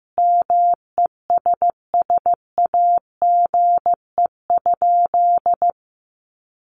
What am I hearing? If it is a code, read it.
Morse code, heard as a keyed tone, about 15 wpm, 700 Hz. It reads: MESSAGE?
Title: Sentences from Top 1100 Words 15wpm